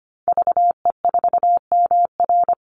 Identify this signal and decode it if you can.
Morse code, heard as a keyed tone, about 25 wpm, 700 Hz. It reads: VE4MR
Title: Call Signs Repeated 3x - 25wpm